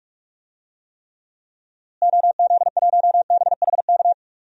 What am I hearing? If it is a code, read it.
Morse code, heard as a keyed tone, about 46 wpm, 700 Hz. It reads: OZ1BHK
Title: DXCC-100 Call Signs - 6 Characters - Repeated 3x - 45wpm